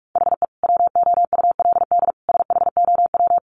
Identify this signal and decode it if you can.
Morse code, heard as a keyed tone, about 44 wpm, 700 Hz. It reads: HE WOULD SHOW
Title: Sentences from Top 1300 Words 45wpm